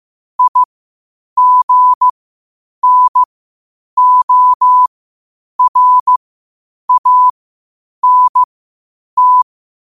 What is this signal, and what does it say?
Morse code, heard as a keyed tone, about 15 wpm, spaced as for 8 wpm, 1000 Hz. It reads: IGNORANT
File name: qotdfiles / 8wpm